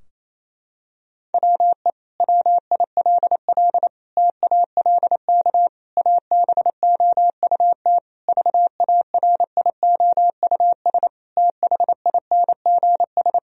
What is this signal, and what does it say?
Morse code, heard as a keyed tone, about 28 wpm, 700 Hz. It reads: WE WILL TALK ABOUT VARIOUS THINGS
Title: Sentences from Top 1000 Words - Review Part 4 - 28wpm